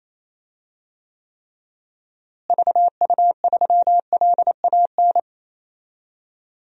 Morse code, heard as a keyed tone, about 28 wpm, 700 Hz. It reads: VU3LAN
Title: DXCC-100 Call Signs - 6 Characters - Repeated 3x - 28wpm